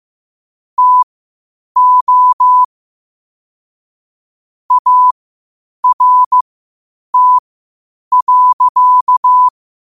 Morse code, heard as a keyed tone, about 15 wpm, spaced as for 8 wpm, 1000 Hz. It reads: TO ART.
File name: qotdfiles / 8wpm